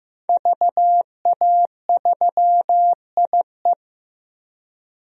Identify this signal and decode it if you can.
Morse code, heard as a keyed tone, about 15 wpm, 700 Hz. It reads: VA3IE
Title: DXCC-100 Call Signs - 5 Characters - Repeated 2x - 15wpm